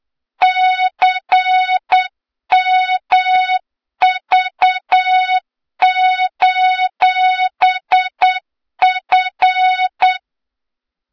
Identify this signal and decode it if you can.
Morse code, heard as a keyed tone, about 8 wpm, 3700 Hz, with noise. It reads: CMV:F
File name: Morze-HA2MN / hang / 23 GYAKORLAT